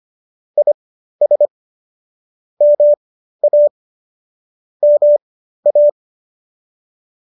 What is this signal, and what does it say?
Morse code, heard as a keyed tone, about 25 wpm, spaced as for 13 wpm, 590 Hz. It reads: IS MA MA